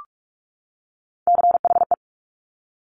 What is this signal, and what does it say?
Morse code, heard as a keyed tone, about 46 wpm, 690 Hz. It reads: CHE
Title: Top Three Letter Combinations in Words 45wpm